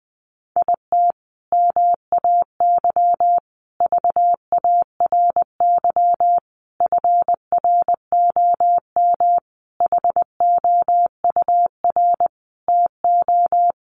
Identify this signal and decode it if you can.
Morse code, heard as a keyed tone, about 20 wpm, 700 Hz. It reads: IT MAY VARY FROM HOUR TO
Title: Sentences from Top 1200 Words 20wpm